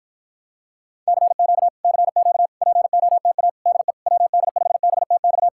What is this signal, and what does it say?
Morse code, heard as a keyed tone, about 54 wpm, 700 Hz. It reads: CQ CQ POTA DE WD5BTX